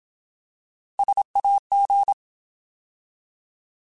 Morse code, heard as a keyed tone, about 27 wpm, 800 Hz. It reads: SAG
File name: Alphabetical / Alphabetical mp3s / Ss @25